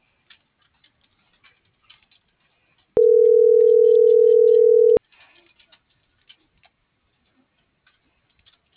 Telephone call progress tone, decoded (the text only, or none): ringback tone